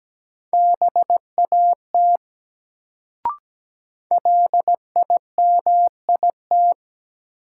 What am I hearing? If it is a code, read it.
Morse code, heard as a keyed tone, about 17 wpm, 700 Hz. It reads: BAT LIMIT
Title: Sentences from Top 1500 Words - New Word Drill - 17wpm